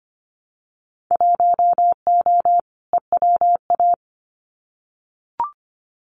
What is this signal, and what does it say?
Morse code, heard as a keyed tone, about 25 wpm, 690 Hz. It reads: 1O EWA
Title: Winter Field Day Exchanges Encore 25wpm